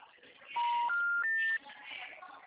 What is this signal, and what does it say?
Special information tone, with speech.